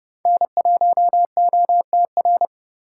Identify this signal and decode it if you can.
Morse code, heard as a keyed tone, about 27 wpm, 700 Hz. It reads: N1OTR